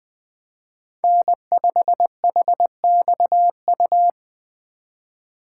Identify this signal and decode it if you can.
Morse code, heard as a keyed tone, about 20 wpm, 700 Hz. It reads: N5HXU
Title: Speed Racing - USA Call Signs 17wpm